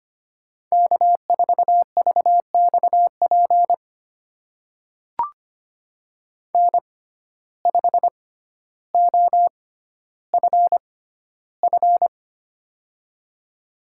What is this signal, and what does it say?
Morse code, heard as a keyed tone, about 25 wpm, spaced as for 13 wpm, 700 Hz. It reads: K4VXP N5OFF
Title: USA Call Signs - 5 Characters - 6x Character Spacing - 25wpm